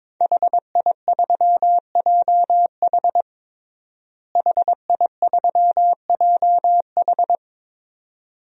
Morse code, heard as a keyed tone, about 22 wpm, 700 Hz. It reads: HI3JH HI3JH
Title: DXCC-100 Call Signs - 5 Characters - Repeated 2x - 22wpm